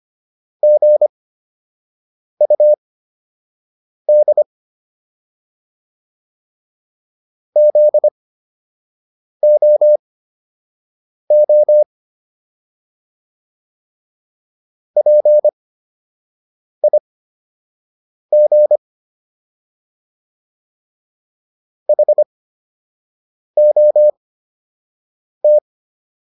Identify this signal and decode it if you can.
Morse code, heard as a keyed tone, about 25 wpm, spaced as for 6 wpm, 600 Hz. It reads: GUD ZOO PIG HOT